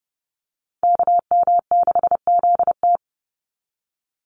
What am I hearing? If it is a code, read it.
Morse code, heard as a keyed tone, about 30 wpm, 700 Hz. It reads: KM6ZT
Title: Call Signs - 30wpm